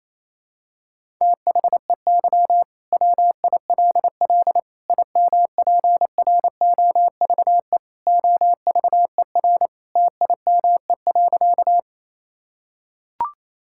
Morse code, heard as a keyed tone, about 28 wpm, 700 Hz. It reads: THEY WILL IMPROVE OVER TIME.
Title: Sentences from Top 1000 Words - Review Part 5 - 28wpm